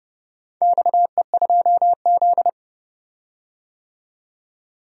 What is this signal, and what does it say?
Morse code, heard as a keyed tone, about 29 wpm, 700 Hz. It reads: XE2Z
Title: DXCC-100 Call Signs - 4 Characters - 30wpm